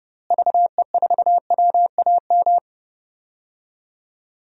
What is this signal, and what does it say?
Morse code, heard as a keyed tone, about 30 wpm, 700 Hz. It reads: VE4WAM
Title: DXCC-100 Call Signs - 6 Characters - Repeated 3x - 30wpm